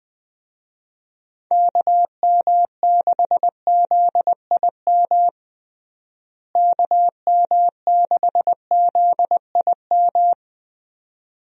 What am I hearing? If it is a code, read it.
Morse code, heard as a keyed tone, about 20 wpm, 700 Hz. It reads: KM6ZIM KM6ZIM
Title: USA Call Signs - 6 Characters - Repeated 2x - 20wpm